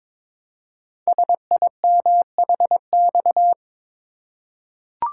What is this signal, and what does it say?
Morse code, heard as a keyed tone, about 22 wpm, 700 Hz. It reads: SIMHX
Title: Code Groups - 5 Letters - 22wpm